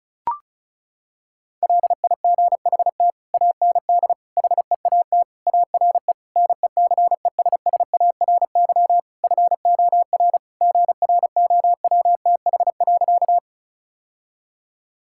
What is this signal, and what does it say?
Morse code, heard as a keyed tone, about 36 wpm, 700 Hz. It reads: LIGHT AND HEAT ARE NECESSARY FOR GROWTH.
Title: Sentences from Top 1000 Words - Review Part 5 - 35wpm